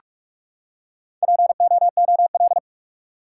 Morse code, heard as a keyed tone, about 43 wpm, 700 Hz. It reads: POOL
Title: Four Letter Words - 45wpm